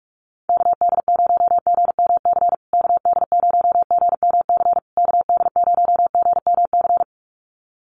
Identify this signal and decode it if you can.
Morse code, heard as a keyed tone, about 48 wpm, 700 Hz. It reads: KD0GMC KD0GMC KD0GMC
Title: USA Call Signs Repeated 3x 45wpm